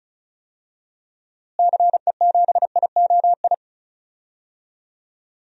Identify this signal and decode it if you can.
Morse code, heard as a keyed tone, about 34 wpm, 700 Hz. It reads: CE7IOI
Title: DXCC-100 Call Signs - 6 Characters - Repeated 3x - 35wpm